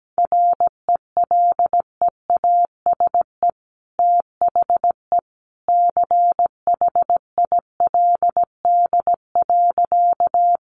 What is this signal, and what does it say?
Morse code, heard as a keyed tone, about 17 wpm, 700 Hz. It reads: RELEASE THE CHILD.
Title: Sentences from Top 1100 Words 17wpm